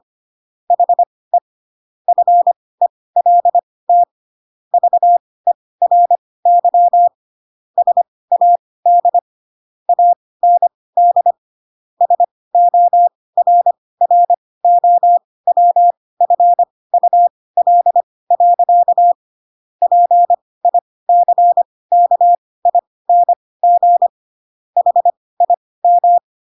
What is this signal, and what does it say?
Morse code, heard as a keyed tone, about 25 wpm, spaced as for 18 wpm, 700 Hz. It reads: HE FELT VERY SAD AND SORROWFUL. PICKING HIM
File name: Pinocchio / Pinocchio ch7 12 25 ik1hge